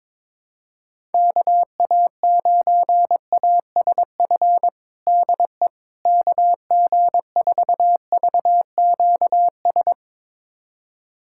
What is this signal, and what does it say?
Morse code, heard as a keyed tone, about 21 wpm, 700 Hz. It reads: KA9ASF DE KG4VQS
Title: USA Callsign DE Callsign 22wpm